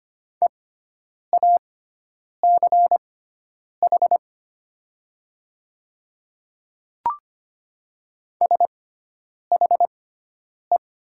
Morse code, heard as a keyed tone, about 25 wpm, spaced as for 9 wpm, 700 Hz. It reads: EACH SHE
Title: 6x Character Spacing - Top 200 Words - 25wpm